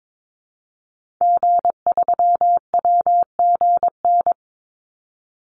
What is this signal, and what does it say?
Morse code, heard as a keyed tone, about 22 wpm, 700 Hz. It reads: G3WGN
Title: Call Signs - 22wpm